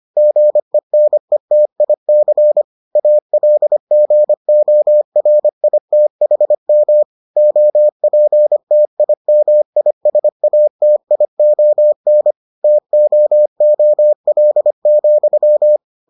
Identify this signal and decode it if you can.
Morse code, heard as a keyed tone, about 25 wpm, 600 Hz. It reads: GENETIC ALGORITHM OPTIMISATION TOOL,